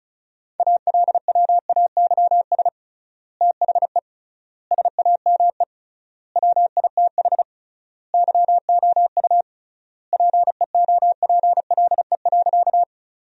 Morse code, heard as a keyed tone, about 36 wpm, 700 Hz. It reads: ALWAYS THE SAME WITH YOU PEOPLE.
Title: 3x Word-Spacing - Sentences from Top 300 Words - 35wpm